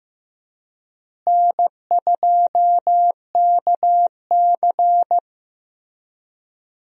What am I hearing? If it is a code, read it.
Morse code, heard as a keyed tone, about 15 wpm, 700 Hz. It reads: N2KC
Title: USA Call Signs - 4 Characters - Repeated 3x - 15wpm